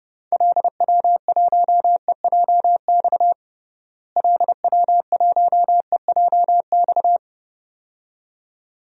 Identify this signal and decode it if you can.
Morse code, heard as a keyed tone, about 30 wpm, 700 Hz. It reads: LW1EJX LW1EJX
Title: DXCC-100 Call Signs - 6 Characters - Repeated 2x - 30wpm